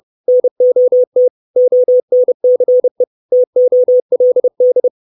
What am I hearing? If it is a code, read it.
Morse code, heard as a keyed tone, about 30 wpm, 500 Hz. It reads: NOT ONCE TOLD
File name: 632 Words 30WPM